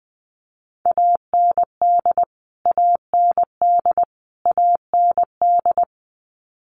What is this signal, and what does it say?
Morse code, heard as a keyed tone, about 20 wpm, 700 Hz. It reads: AND AND AND
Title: Top 200 Words Repeated 20wpm